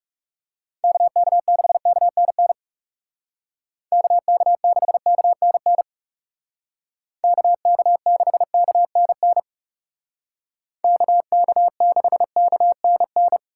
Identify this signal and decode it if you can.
Morse code, heard as a keyed tone, about 37 wpm, 700 Hz. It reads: KK6KNN KK6KNN KK6KNN KK6KNN
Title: Speed Racing - USA 2x3 Call Signs 30wpm